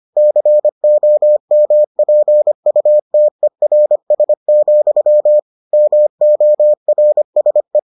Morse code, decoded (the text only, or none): COMPUTERS, MORSE